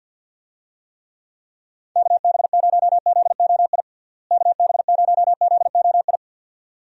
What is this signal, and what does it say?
Morse code, heard as a keyed tone, about 46 wpm, 700 Hz. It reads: KB0ZOI KB0ZOI
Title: USA Call Signs - 6 Characters - Repeated 2x - 50wpm